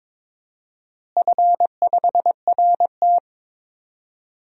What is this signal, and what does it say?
Morse code, heard as a keyed tone, about 22 wpm, 700 Hz. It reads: F5RT
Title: DXCC-100 Call Signs - 4 Characters - Repeated 3x - 22wpm